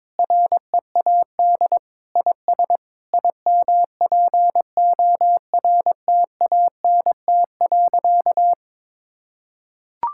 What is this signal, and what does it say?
Morse code, heard as a keyed tone, about 22 wpm, 700 Hz. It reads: READ IS IMPORTANT.
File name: practice / 22wpm / Sentences with 3-Word Phrases - 22wpm